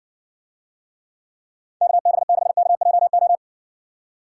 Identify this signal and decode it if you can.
Morse code, heard as a keyed tone, about 68 wpm, 700 Hz. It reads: KB6KJO